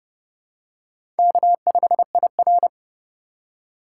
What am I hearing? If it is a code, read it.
Morse code, heard as a keyed tone, about 30 wpm, 690 Hz. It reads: K5IR
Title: USA Call Signs - 3x Character Spacing - 30wpm